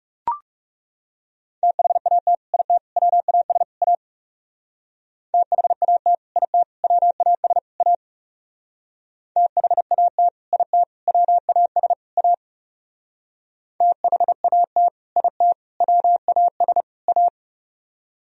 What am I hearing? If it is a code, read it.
Morse code, heard as a keyed tone, about 36 wpm, 700 Hz. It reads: THAT IT WAS A THAT IT WAS A THAT IT WAS A THAT IT WAS A
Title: Speed-Racing - Sets of 4 Words 30wpm